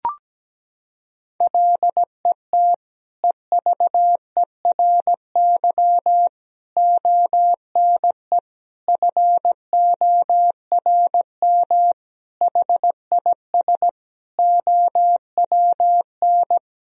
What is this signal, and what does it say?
Morse code, heard as a keyed tone, about 17 wpm, 700 Hz. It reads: LET EVERY ONE FORM HIS OWN
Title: Sentences from Top 1000 Words - Review Part 1 - 17wpm